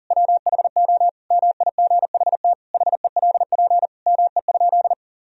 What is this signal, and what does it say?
Morse code, heard as a keyed tone, about 40 wpm, 700 Hz. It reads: WHO MIGHT HELP ME?
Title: Sentences from Top 1000 Words - Review Part 3 - 40wpm